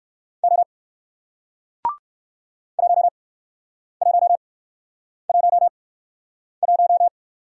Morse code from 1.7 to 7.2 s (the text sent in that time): 1111